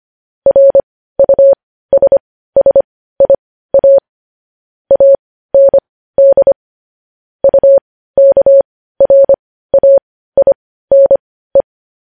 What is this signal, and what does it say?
Morse code, heard as a keyed tone, about 25 wpm, spaced as for 15 wpm, 550 Hz. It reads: RUSSIA AND UKRAINE